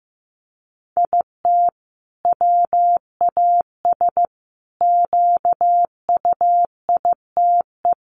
Morse code, heard as a keyed tone, about 15 wpm, 700 Hz. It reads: IT WAS QUITE
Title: Sentences from Top 1500 Words 15wpm